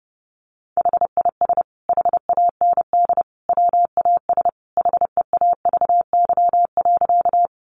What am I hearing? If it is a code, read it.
Morse code, heard as a keyed tone, about 30 wpm, 700 Hz. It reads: HIS HAND WAS HEAVY.
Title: Sentences from Top 1100 Words 30wpm